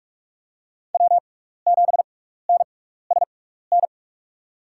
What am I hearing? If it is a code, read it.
Morse code, heard as a keyed tone, about 44 wpm, spaced as for 15 wpm, 700 Hz. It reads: W7NSN
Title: USA Call Signs - 5 Characters - 6x Character Spacing - 45wpm